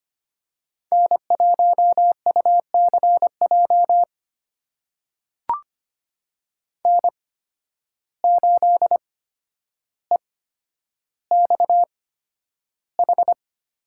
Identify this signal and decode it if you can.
Morse code, heard as a keyed tone, about 25 wpm, spaced as for 11 wpm, 700 Hz. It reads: N1UCJ N8EXH
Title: USA Call Signs - 5 Characters - 8x Character Spacing - 25wpm